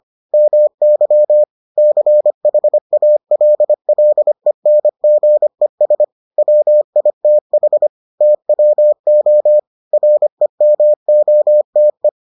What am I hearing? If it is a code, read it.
Morse code, heard as a keyed tone, about 25 wpm, 600 Hz. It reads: MY CHALLENGES WITH TWO REMOTE